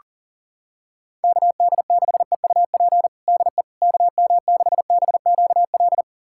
Morse code, heard as a keyed tone, about 37 wpm, 700 Hz. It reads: KD6EUP DE KM6BQL
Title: USA Callsign DE Callsign - 6 Characters - 40wpm